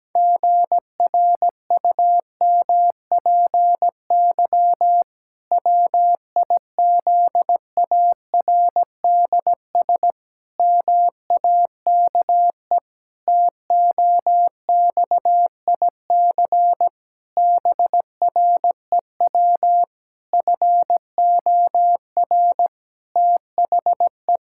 Morse code, heard as a keyed tone, about 17 wpm, 700 Hz. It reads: GRUMPY WIZARDS MAKE TOXIC BREW FOR THE
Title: Pangram Challenge v2 - 17wpm